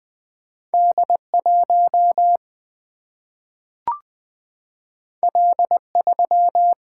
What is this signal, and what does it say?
Morse code, heard as a keyed tone, about 20 wpm, 700 Hz. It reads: D1 L3
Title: Single-Letter Single-Number 20wpm